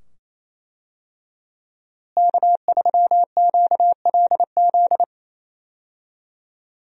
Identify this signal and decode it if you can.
Morse code, heard as a keyed tone, about 27 wpm, 700 Hz. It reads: K3QLZ
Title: USA Call Signs - 5 Characters - Repeated 2x - 28wpm